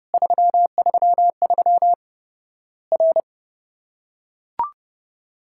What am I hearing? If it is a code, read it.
Morse code, heard as a keyed tone, about 30 wpm, 700 Hz. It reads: 333 R
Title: Worked All Europe DX Contest - QTC Exchange - 30wpm